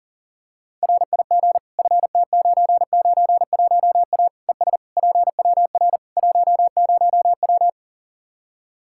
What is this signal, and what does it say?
Morse code, heard as a keyed tone, about 38 wpm, 700 Hz. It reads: RIG FT991A ES PWR 10W
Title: Practice QSO - Exchange 2 - Info - 40wpm